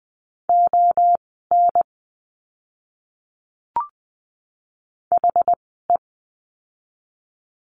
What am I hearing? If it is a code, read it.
Morse code, heard as a keyed tone, about 20 wpm, spaced as for 15 wpm, 700 Hz. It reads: ON HE
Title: 206 Lesson 33f - Letter j - Words 20wpm